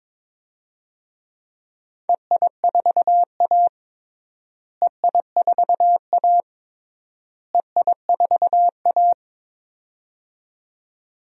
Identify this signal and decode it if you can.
Morse code, heard as a keyed tone, about 21 wpm, 700 Hz. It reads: EI4A EI4A EI4A
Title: DXCC-100 Call Signs - 4 Characters - Repeated 3x - 22wpm